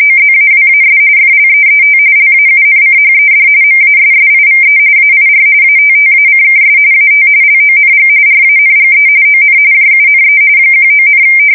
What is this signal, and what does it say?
RTTY, about 45.45 baud. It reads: OZAWAS FLEET LAUNCHED MASSIVE AIR ATTACKS AT LONG RANGE ON JUNE 19,
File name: fsk170 / battleshipnj